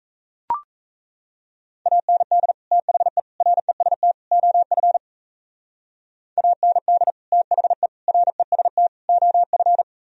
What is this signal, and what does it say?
Morse code, heard as a keyed tone, about 35 wpm, 700 Hz. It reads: AND THE REST OF AND THE REST OF